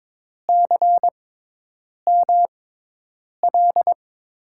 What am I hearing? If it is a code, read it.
Morse code, heard as a keyed tone, about 22 wpm, spaced as for 8 wpm, 700 Hz. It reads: CML
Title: USA Call Signs - 5 Characters - 6x Character Spacing - 22wpm